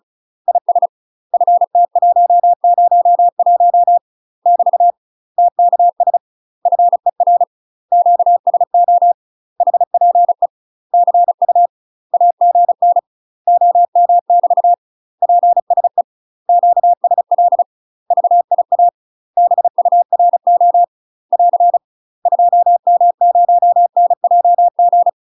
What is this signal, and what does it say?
Morse code, heard as a keyed tone, about 35 wpm, 700 Hz. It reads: IS FT101 <BT> TKS FER QSO HPE CU AGN OM <BT> PSE QSL VIA BURO <AR> 2M0NJG
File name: UK MASTER QSOs TRAINING / 35WPM / UK MASTER QSOs 0033